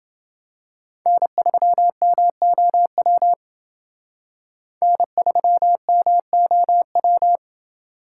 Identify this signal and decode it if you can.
Morse code, heard as a keyed tone, about 29 wpm, 700 Hz. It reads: N3MOW N3MOW